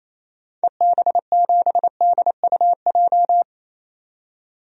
Morse code, heard as a keyed tone, about 27 wpm, 700 Hz. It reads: EB7DUJ
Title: DXCC-100 Call Signs - 6 Characters - 28wpm